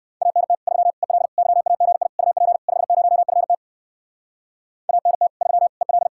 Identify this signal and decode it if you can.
Morse code, heard as a keyed tone, about 56 wpm, 700 Hz. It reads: ANT 3 EL YAGI UP 50FT ANT 3 EL
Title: Speed-Racing Common QSO Phrases - 45wpm